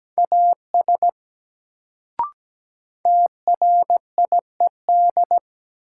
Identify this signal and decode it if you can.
Morse code, heard as a keyed tone, about 17 wpm, 700 Hz. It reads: AS TRIED
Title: Sets of 2 Words - New Word Drill - 17wpm